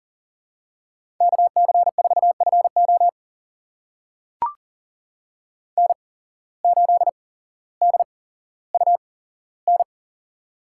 Morse code, heard as a keyed tone, about 40 wpm, spaced as for 17 wpm, 700 Hz. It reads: KC4FO N8DUN
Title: USA Call Signs - 5 Characters - 8x Character Spacing - 40wpm